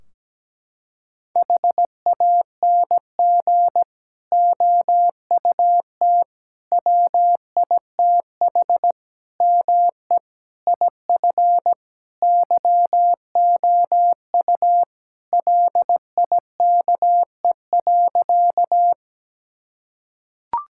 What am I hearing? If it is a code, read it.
Morse code, heard as a keyed tone, about 17 wpm, 700 Hz. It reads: HANG OUT WITH ME IF YOU LIKE.
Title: Sentences from Top 1600 Words 17wpm